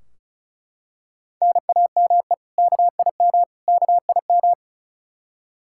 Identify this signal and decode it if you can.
Morse code, heard as a keyed tone, about 36 wpm, 700 Hz. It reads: NAME KIM KIM
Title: K1USN SST QSO Practice 35wpm